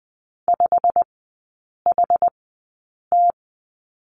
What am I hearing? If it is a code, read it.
Morse code, heard as a keyed tone, about 20 wpm, spaced as for 8 wpm, 700 Hz. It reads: 5HT
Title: 107 Lesson 18e - Letter m - Three Characters 20wpm